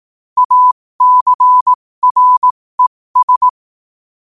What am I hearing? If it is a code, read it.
Morse code, heard as a keyed tone, about 18 wpm, spaced as for 15 wpm, 1000 Hz. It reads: ACRES
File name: qotdfiles / 15wpm